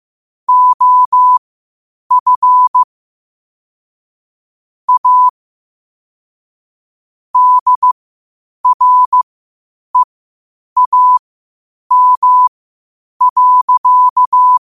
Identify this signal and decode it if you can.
Morse code, heard as a keyed tone, about 15 wpm, spaced as for 8 wpm, 1000 Hz. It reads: OF A DREAM.
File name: qotdfiles / 8wpm